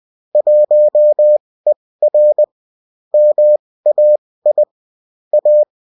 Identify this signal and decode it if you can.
Morse code, heard as a keyed tone, about 20 wpm, spaced as for 16 wpm, 600 Hz. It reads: 1ER MAI A